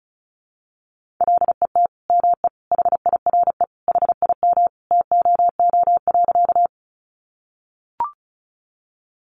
Morse code, heard as a keyed tone, about 34 wpm, 700 Hz. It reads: LET ME HIRE HIM TOO.
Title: Sentences from Top 1400 Words 35wpm